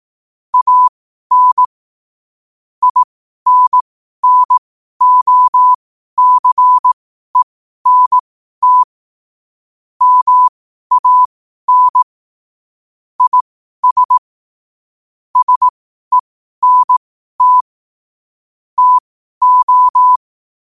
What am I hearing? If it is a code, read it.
Morse code, heard as a keyed tone, about 18 wpm, spaced as for 12 wpm, 1000 Hz. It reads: AN INNOCENT MAN IS SENT TO
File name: qotdfiles / 12wpm